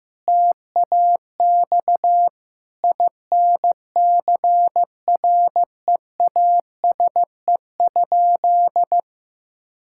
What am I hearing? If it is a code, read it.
Morse code, heard as a keyed tone, about 15 wpm, 700 Hz. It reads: TAX INCREASE?
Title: Sentences from Top 1000 Words - Review Part 5 - 15wpm